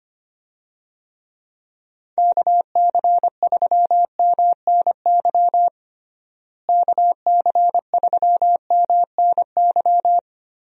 Morse code, heard as a keyed tone, about 25 wpm, 700 Hz. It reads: KC3MNY KC3MNY
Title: USA Call Signs - 6 Characters - Repeated 2x - 25wpm